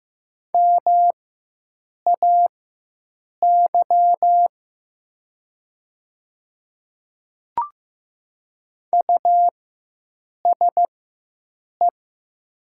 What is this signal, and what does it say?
Morse code, heard as a keyed tone, about 15 wpm, spaced as for 7 wpm, 700 Hz. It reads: MAY USE
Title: 4x Character Spacing - Top 300 Words - 15wpm